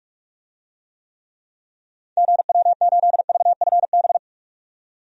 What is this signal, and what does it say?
Morse code, heard as a keyed tone, about 44 wpm, 700 Hz. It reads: GW8VFB
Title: DXCC-100 Call Signs - 6 Characters - 45wpm